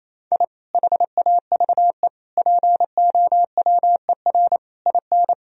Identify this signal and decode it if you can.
Morse code, heard as a keyed tone, about 28 wpm, 700 Hz. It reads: I HAVE POWER IN
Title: Sentences from Top 1000 Words - Review Part 2 - 28wpm